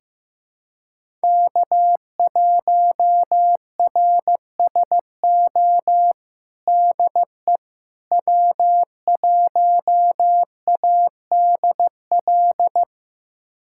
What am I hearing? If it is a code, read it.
Morse code, heard as a keyed tone, about 16 wpm, 700 Hz. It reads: K1RSO DE W1ADL
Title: USA Callsign DE Callsign - 5 Characters - 15wpm